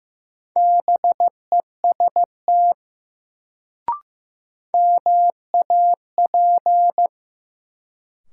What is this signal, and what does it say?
Morse code, heard as a keyed tone, about 15 wpm, 700 Hz. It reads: BEST MAP
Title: Sentences from Top 400 Words - New Word Drill - 15wpm